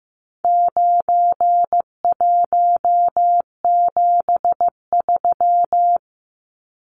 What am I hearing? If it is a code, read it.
Morse code, heard as a keyed tone, about 15 wpm, 700 Hz. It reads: 9173
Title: Numbers - 4-Digit 15wpm